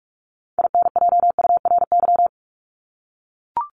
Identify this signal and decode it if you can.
Morse code, heard as a keyed tone, about 44 wpm, 700 Hz. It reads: INJURY
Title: Top 1751-2000 Words 45wpm